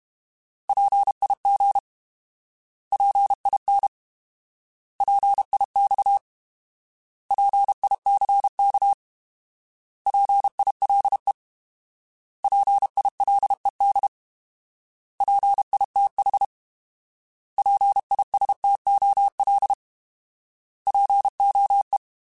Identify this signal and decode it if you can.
Morse code, heard as a keyed tone, about 32 wpm, 800 Hz. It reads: PIG PIN PIX PICK PILE PILED PITH PISTOL POE